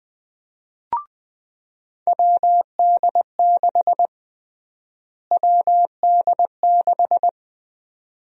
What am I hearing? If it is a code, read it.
Morse code, heard as a keyed tone, about 20 wpm, 700 Hz. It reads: WD6 WD6
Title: USA Call Sign Prefixes - Repeated 2x - 20wpm